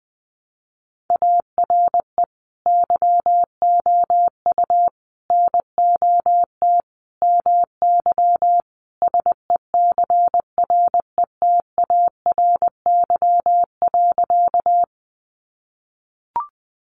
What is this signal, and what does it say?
Morse code, heard as a keyed tone, about 20 wpm, 700 Hz. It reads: ARE YOU NOT MY SECRETARY.
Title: Sentences from Top 1300 Words 20wpm